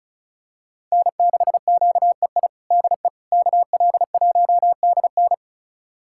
Morse code, heard as a keyed tone, about 36 wpm, 700 Hz. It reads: N6QEI DE KL1DN
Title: USA Callsign DE Callsign - 5 Characters - 35wpm